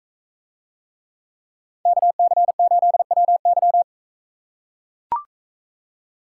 Morse code, heard as a keyed tone, about 42 wpm, 700 Hz. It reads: KC8WY